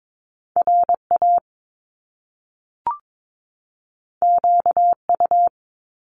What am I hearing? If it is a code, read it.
Morse code, heard as a keyed tone, about 22 wpm, 700 Hz. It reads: RA QU
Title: Top Two Letter Combinations in Words 22wpm